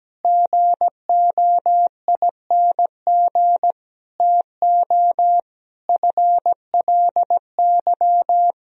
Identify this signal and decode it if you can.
Morse code, heard as a keyed tone, about 17 wpm, 700 Hz. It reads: GOING TO FLY
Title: Sentences from Top 1000 Words - Review Part 2 - 17wpm